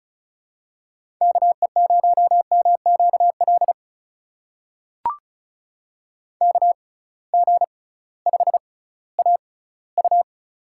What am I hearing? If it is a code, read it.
Morse code, heard as a keyed tone, about 36 wpm, spaced as for 19 wpm, 700 Hz. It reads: KE0MQL KG5AU
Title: USA Call Signs - 6x Character Spacing - 35wpm